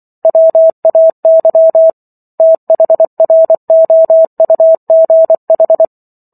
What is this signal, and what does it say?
Morse code, heard as a keyed tone, about 24 wpm, 650 Hz. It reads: WAY THROUGH